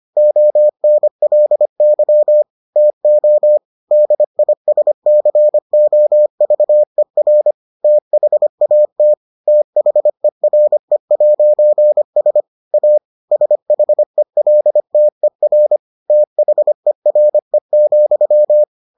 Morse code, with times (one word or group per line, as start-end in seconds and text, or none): 0.2-2.4 ONLY
2.8-3.6 TO
3.9-7.5 DISCOVER
7.8-9.1 THAT
9.5-12.4 THERE'S
12.7-13.0 A
13.3-15.8 SHELTER
16.1-18.6 THERE,